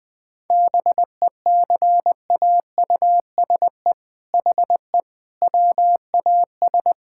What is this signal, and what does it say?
Morse code, heard as a keyed tone, about 20 wpm, 700 Hz. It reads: BECAUSE HE WAS
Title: All-Age Appropriate Jokes - 20wpm